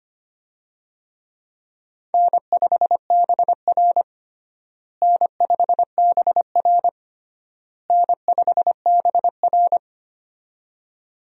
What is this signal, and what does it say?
Morse code, heard as a keyed tone, about 25 wpm, 700 Hz. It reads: N5BR N5BR N5BR
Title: USA Call Signs - 4 Characters - Repeated 3x - 25wpm